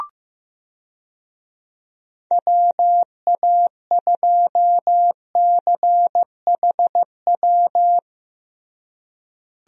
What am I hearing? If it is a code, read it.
Morse code, heard as a keyed tone, about 15 wpm, 700 Hz. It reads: WA2CHW
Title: USA Call Signs - 6 Characters - Repeated 3x - 15wpm